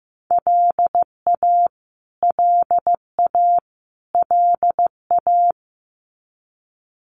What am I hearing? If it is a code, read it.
Morse code, heard as a keyed tone, about 15 wpm, 700 Hz. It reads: LA LA LA
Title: US States Repeated 3x 15wpm